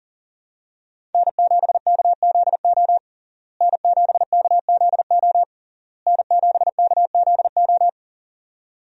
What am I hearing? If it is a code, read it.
Morse code, heard as a keyed tone, about 37 wpm, 700 Hz. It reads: N7KZO N7KZO N7KZO
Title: USA Call Signs - 5 Characters - Repeated 3x - 40wpm